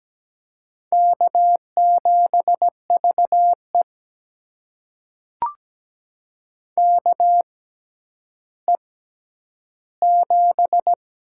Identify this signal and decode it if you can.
Morse code, heard as a keyed tone, about 17 wpm, spaced as for 10 wpm, 700 Hz. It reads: K7VE KE7
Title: USA Call Signs - 4 Characters - 6x Character Spacing - 17wpm